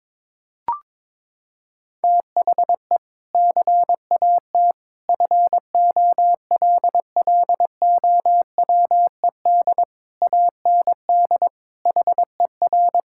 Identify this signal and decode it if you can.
Morse code, heard as a keyed tone, about 22 wpm, 700 Hz. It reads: THE CAT FOLLOWED AND HER
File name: practice / 22wpm / Sentences with 2-Word Phrases - 22wpm